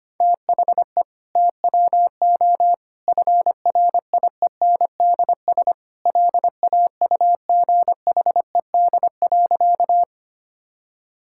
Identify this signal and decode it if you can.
Morse code, heard as a keyed tone, about 25 wpm, 700 Hz. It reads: THE TWO FRIENDS LAUGHED.
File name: practice / 25wpm / Sentences with 2-Word Phrases - 25wpm